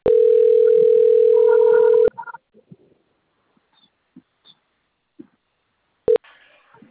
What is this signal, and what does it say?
Ringback tone (North America), 440 + 480 Hz.
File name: out- - - - - .7497